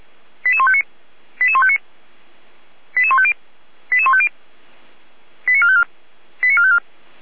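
Five-tone selective call (ZVEI), heard as quick runs of five tones, tones of about 70 ms. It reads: 88159 88159 88150 88150 89454 89454
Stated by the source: ZVEI five-tone selective call received with an SDR.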